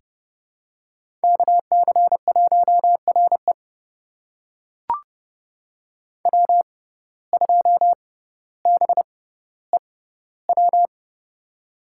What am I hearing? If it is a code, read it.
Morse code, heard as a keyed tone, about 30 wpm, spaced as for 15 wpm, 690 Hz. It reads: KC1RE W2BEW
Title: USA Call Signs - 5 Characters - 6x Character Spacing - 30wpm